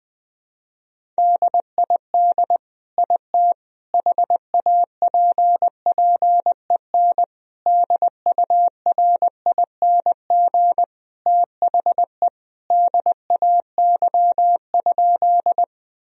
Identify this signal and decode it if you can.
Morse code, heard as a keyed tone, about 19 wpm, 700 Hz. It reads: DID IT HAPPEN DURING THE DAY?
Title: Sentences from Top 1000 Words - Review Part 2 - 20wpm